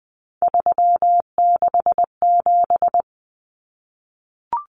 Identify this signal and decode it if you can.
Morse code, heard as a keyed tone, about 20 wpm, 700 Hz. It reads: 367
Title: Common Cut Numbers - 3-Digit 20wpm